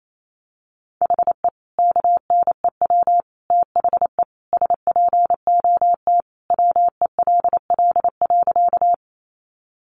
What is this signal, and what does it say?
Morse code, heard as a keyed tone, about 28 wpm, 700 Hz. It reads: HE KNEW THE SPOT WELL.
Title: Sentences from Top 1100 Words 28wpm